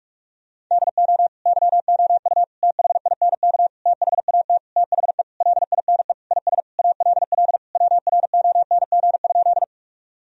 Morse code, heard as a keyed tone, about 46 wpm, 700 Hz. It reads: DO YOU THINK THAT THE LINE IS ALL WRONG?
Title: Sentences from Top 1000 Words - Review Part 1 - 45wpm